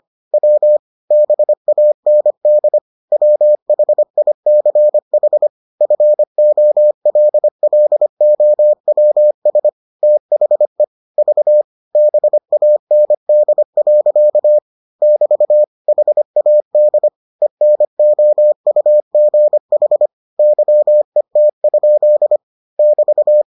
Morse code, heard as a keyed tone, about 25 wpm, 600 Hz. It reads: W BAND WHICH FOLLOWS THE V BAND. <BT> HAD ENOUGH YET? <BT>